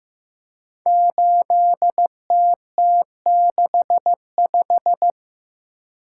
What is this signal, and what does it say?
Morse code, heard as a keyed tone, about 15 wpm, 700 Hz. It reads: 8TT65
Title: Code Groups - 5 Letter:Numbers - 15wpm